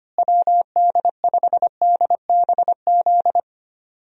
Morse code, heard as a keyed tone, about 25 wpm, 700 Hz. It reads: WD5DBZ
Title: USA Call Signs - 6 Characters - Repeated 2x - 25wpm